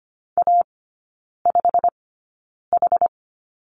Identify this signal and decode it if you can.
Morse code, heard as a keyed tone, about 24 wpm, spaced as for 9 wpm, 690 Hz. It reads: A5H
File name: Lesson 15e - Number 5 - Three Characters 25wpm